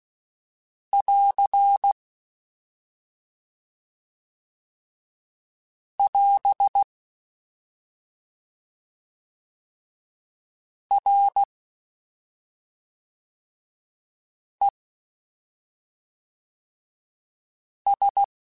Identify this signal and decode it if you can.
Morse code, heard as a keyed tone, about 16 wpm, 800 Hz. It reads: <AR> <AS> R E S